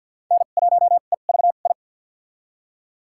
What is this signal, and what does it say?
Morse code, heard as a keyed tone, about 53 wpm, spaced as for 37 wpm, 700 Hz. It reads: N1EVI
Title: USA Call Signs - 5 Characters - 2x Character Spacing - 50wpm